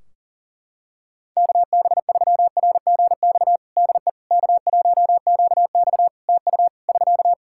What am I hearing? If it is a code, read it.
Morse code, heard as a keyed tone, about 40 wpm, 700 Hz. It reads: KB3RGX DE K1QX TU <SK>
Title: Practice QSO - Exchange 4 - Ending - 40wpm